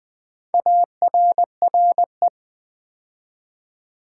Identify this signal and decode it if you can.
Morse code, heard as a keyed tone, about 20 wpm, 700 Hz. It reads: ARRE
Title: Build A Word - 2 Letters At A Time - 6-Letter Words - 20wpm